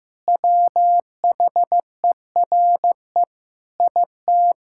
Morse code, heard as a keyed tone, about 16 wpm, 700 Hz. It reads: WHERE IT
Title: Sentences from Top 1600 Words 15wpm